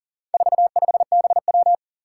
Morse code, heard as a keyed tone, about 40 wpm, 700 Hz. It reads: 45BW
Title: Four Challenging Letter-Number Combinations 40wpm